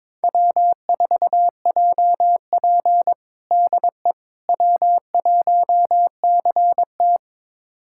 Morse code, heard as a keyed tone, about 22 wpm, 700 Hz. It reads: W4JP DE W1CT
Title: USA Callsign DE Callsign - 4 Characters - 22wpm